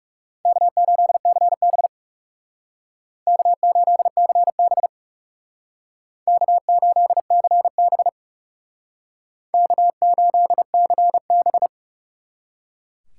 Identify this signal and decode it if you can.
Morse code, heard as a keyed tone, about 36 wpm, 700 Hz. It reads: K8CB K8CB K8CB K8CB
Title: Speed Racing - USA 1x2 Call Signs 30wpm